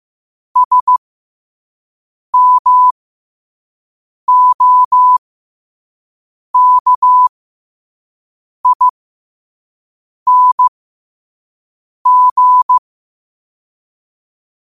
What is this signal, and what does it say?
Morse code, heard as a keyed tone, about 15 wpm, spaced as for 5 wpm, 1000 Hz. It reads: SMOKING
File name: qotdfiles / 5wpm